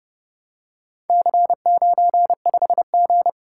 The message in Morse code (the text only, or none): C95G